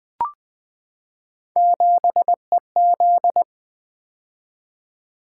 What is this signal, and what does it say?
Morse code, heard as a keyed tone, about 20 wpm, 700 Hz. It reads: 7EZ
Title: Three Letter-Number Combinations - 20wpm